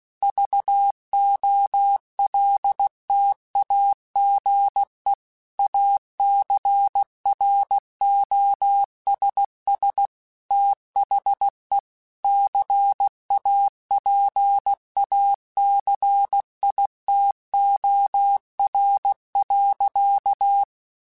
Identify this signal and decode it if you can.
Morse code, heard as a keyed tone, about 16 wpm, 800 Hz. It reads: VOLTAGE ACROSS THE CAPACITOR.
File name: Technical Subjects / Technical Subjects-mp3s / Reactance @15